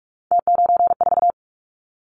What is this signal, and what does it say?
Morse code, heard as a keyed tone, about 44 wpm, 700 Hz. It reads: T94
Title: Callsign Prefixes for Top 50 DXCC - 45wpm